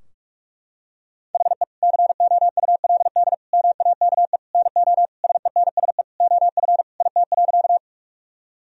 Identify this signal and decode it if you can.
Morse code, heard as a keyed tone, about 43 wpm, 700 Hz. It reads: HE COULD MAKE NO SENSE OF IT.
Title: Sentences from Top 1000 Words - Review Part 4 - 45wpm